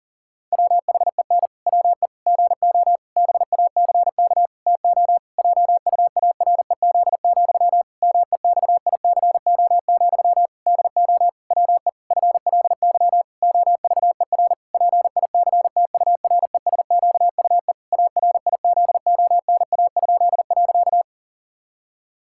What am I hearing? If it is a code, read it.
Morse code, heard as a keyed tone, about 40 wpm, 700 Hz. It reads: WHEN WE GO BACK TO JUAREZ, MEXICO, DO WE FLY OVER PICTURESQUE ARIZONA?.
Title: Pangram Challenge v2 - 40wpm